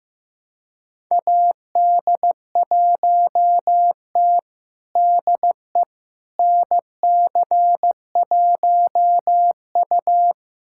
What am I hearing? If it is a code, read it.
Morse code, heard as a keyed tone, about 15 wpm, 700 Hz. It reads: AD1T DE NC1U
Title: USA Callsign DE Callsign - 4 Characters - 15wpm